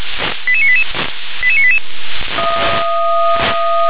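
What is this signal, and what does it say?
Five-tone selective call (ZVEI), heard as quick runs of five tones, tones of about 70 ms. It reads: 88988 88988
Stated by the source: Five-tone ZVEI selective call received on SDR.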